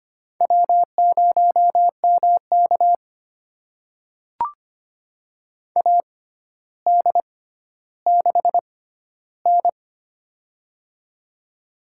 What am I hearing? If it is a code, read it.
Morse code, heard as a keyed tone, about 25 wpm, spaced as for 13 wpm, 700 Hz. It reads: W0MK AD6N
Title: USA Call Signs - 4 Characters - 6x Character Spacing - 25wpm